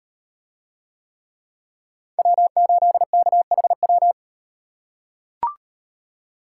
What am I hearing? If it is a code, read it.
Morse code, heard as a keyed tone, about 35 wpm, 700 Hz. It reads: W8KHW